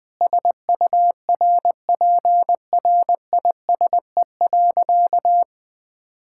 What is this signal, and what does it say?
Morse code, heard as a keyed tone, about 19 wpm, 700 Hz. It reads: SURPRISE.
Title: Sentences with 5-Word Phrases - Spoken Beginning - 20wpm